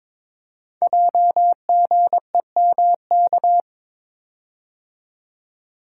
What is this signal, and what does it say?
Morse code, heard as a keyed tone, about 22 wpm, 700 Hz. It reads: JGEMK
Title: Code Groups - 5 Letters - 22wpm